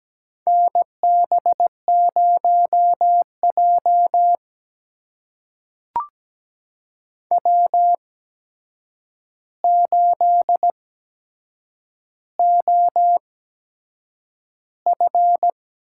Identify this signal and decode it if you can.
Morse code, heard as a keyed tone, about 17 wpm, spaced as for 7 wpm, 700 Hz. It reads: NB0J W8OF
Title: USA Call Signs - 4 Characters - 8x Character Spacing - 17wpm